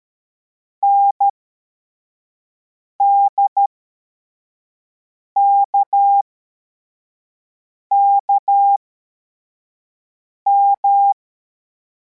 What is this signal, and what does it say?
Morse code, heard as a keyed tone, about 13 wpm, spaced as for 4 wpm, 800 Hz. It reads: NDKKM
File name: formation / cw / Text21